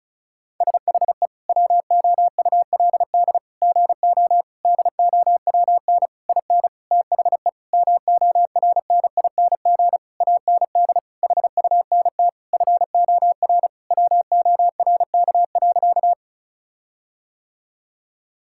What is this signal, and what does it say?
Morse code, heard as a keyed tone, about 36 wpm, 700 Hz. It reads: SHE WOULD GO DOWN IN THE MORNING AND HUNT FOR WORK.
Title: Sentences from Top 1600 Words 35wpm